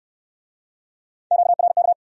Morse code, heard as a keyed tone, about 65 wpm, 700 Hz. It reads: QRK